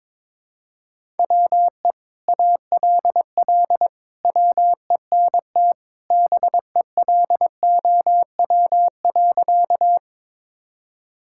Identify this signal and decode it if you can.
Morse code, heard as a keyed tone, about 22 wpm, 700 Hz. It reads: WE ALL WENT BELOW.
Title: Sentences from Top 1000 Words - Review Part 5 - 22wpm